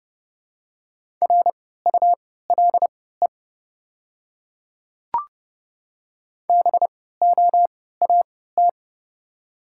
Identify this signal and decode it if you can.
Morse code, heard as a keyed tone, about 30 wpm, spaced as for 17 wpm, 700 Hz. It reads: RULE BOAT
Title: 3x Character Spacing - Top 500 Words - 30wpm